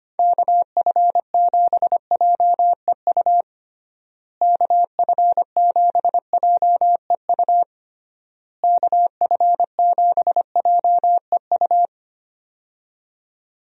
Morse code, heard as a keyed tone, about 25 wpm, 700 Hz. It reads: KF7JEU KF7JEU KF7JEU
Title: USA Call Signs - 6 Characters - Repeated 3x - 25wpm